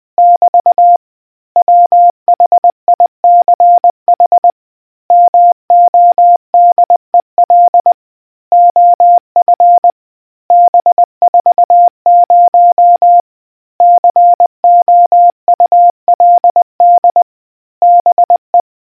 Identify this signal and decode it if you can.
Morse code, heard as a keyed tone, about 20 wpm, 700 Hz. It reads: <BT> WHICH MODEL OF B40 COULD BE